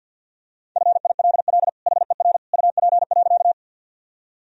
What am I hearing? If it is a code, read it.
Morse code, heard as a keyed tone, about 50 wpm, 700 Hz. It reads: FILL HER UP.
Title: Sentences from Top 1000 Words - Review Part 3 - 50wpm